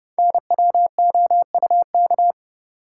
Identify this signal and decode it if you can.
Morse code, heard as a keyed tone, about 30 wpm, 700 Hz. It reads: NWOUK
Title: Code Groups - 5 Letters - 30wpm